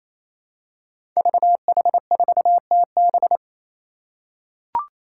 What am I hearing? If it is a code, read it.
Morse code, heard as a keyed tone, about 28 wpm, 700 Hz. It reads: VH4TB
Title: Five Challenging Letter-Number Combinations 28wpm